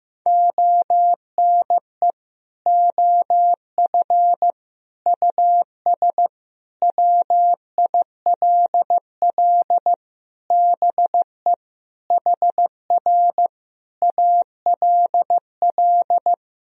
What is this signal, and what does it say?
Morse code, heard as a keyed tone, about 15 wpm, 700 Hz. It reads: ONE OF US WILL BE HR ALL
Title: Sentences from Top 300 Words and Abbreviations 15wpm